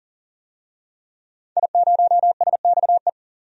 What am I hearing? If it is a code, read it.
Morse code, heard as a keyed tone, about 37 wpm, 700 Hz. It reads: I0SXE
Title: DXCC-100 Call Signs - 5 Characters - 40wpm